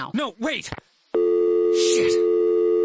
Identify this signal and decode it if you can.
Dial tone (North America), 350 + 440 Hz, with speech.
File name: phone numbers / calls / 911 01 var1